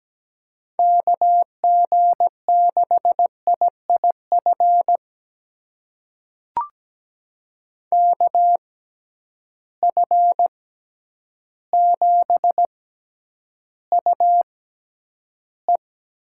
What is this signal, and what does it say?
Morse code, heard as a keyed tone, about 17 wpm, spaced as for 9 wpm, 700 Hz. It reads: KG6IIF KF7UE
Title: USA Call Signs - 6 Characters - 6x Character Spacing - 17wpm